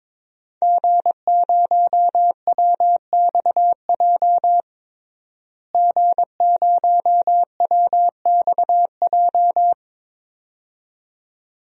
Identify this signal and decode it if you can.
Morse code, heard as a keyed tone, about 22 wpm, 700 Hz. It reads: G0WXJ G0WXJ
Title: DXCC-100 Call Signs - 5 Characters - Repeated 2x - 22wpm